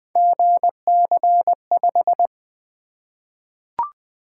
Morse code, heard as a keyed tone, about 20 wpm, 700 Hz. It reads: GC5
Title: Three Letter-Number Combinations - 20wpm